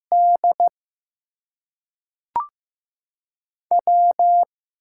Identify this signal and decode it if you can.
Morse code, heard as a keyed tone, about 15 wpm, 700 Hz. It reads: D W
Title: USA Call Signs - 6 Characters - 8x Character Spacing - 15wpm